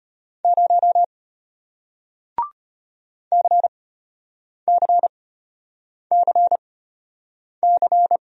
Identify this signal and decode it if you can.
Morse code, heard as a keyed tone, about 34 wpm, spaced as for 8 wpm, 700 Hz. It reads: 0 CCCC